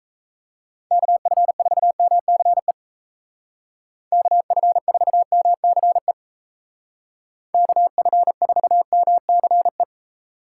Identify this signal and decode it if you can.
Morse code, heard as a keyed tone, about 33 wpm, 700 Hz. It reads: KF4MCE KF4MCE KF4MCE
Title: Speed Racing - USA 2x3 Call Signs 28wpm